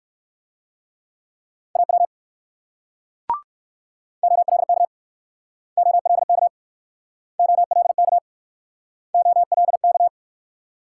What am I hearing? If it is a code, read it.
Morse code, heard as a keyed tone, about 63 wpm, 700 Hz. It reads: AW OLK OLK OLK OLK